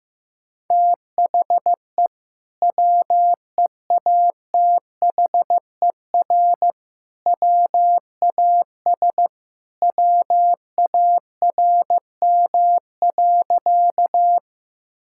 Morse code, heard as a keyed tone, about 16 wpm, 700 Hz. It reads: THE WEATHER WAS WARM.
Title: Sentences from Top 1000 Words - Review Part 5 - 15wpm